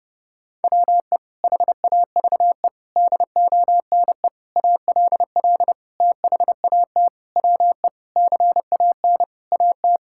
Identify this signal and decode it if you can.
Morse code, heard as a keyed tone, about 28 wpm, 700 Hz. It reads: WE HAVE DONE ALL THAT WE CAN AT
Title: Sentences from Top 1000 Words - Review Part 5 - 30wpm